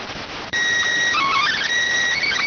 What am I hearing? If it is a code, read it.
SSTV (slow-scan television), PD 290.